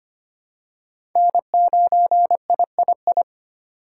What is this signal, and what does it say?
Morse code, heard as a keyed tone, about 25 wpm, 700 Hz. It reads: N9III
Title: USA Call Signs - 5 Characters - 3x Character Spacing - 25wpm